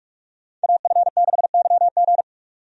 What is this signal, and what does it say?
Morse code, heard as a keyed tone, about 46 wpm, 700 Hz. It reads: AF6YG
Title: USA Call Signs - 5 Characters - 45wpm